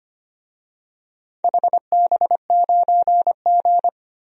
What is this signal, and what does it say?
Morse code, heard as a keyed tone, about 25 wpm, 700 Hz. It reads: HB9G
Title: DXCC-100 Call Signs - 4 Characters - Repeated 2x - 25wpm